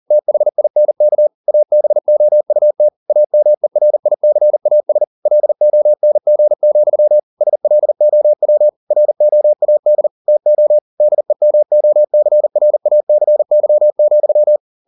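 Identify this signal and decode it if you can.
Morse code, heard as a keyed tone, about 40 wpm, 600 Hz. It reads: THINK ABOUT AMERICAS LONG, SLOW ROAD TO DEMOCRACY,
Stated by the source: Morse code at 40 wpm, practice text, CW med Gnister.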